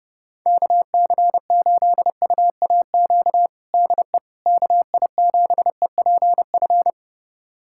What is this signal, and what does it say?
Morse code, heard as a keyed tone, about 30 wpm, 690 Hz. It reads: KC8UAQ DE KI7EPF
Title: USA Callsign DE Callsign - 6 Characters - 30wpm